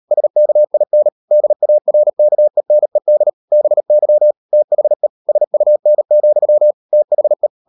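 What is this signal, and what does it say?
Morse code, heard as a keyed tone, about 38 wpm, 600 Hz. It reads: SKIN DARKENED BY THE SUN, THE
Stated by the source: Morse code practice at 38 WPM, sending Ken Follett's novel Never in English, from CW med Gnister.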